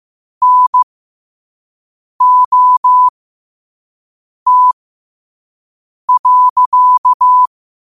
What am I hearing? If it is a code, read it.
Morse code, heard as a keyed tone, about 15 wpm, spaced as for 5 wpm, 1000 Hz. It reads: NOT.
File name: qotdfiles / 5wpm